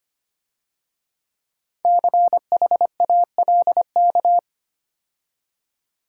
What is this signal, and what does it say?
Morse code, heard as a keyed tone, about 30 wpm, 700 Hz. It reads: CHALK
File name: Easy Words - Speed-Racing with Hints - 25wpm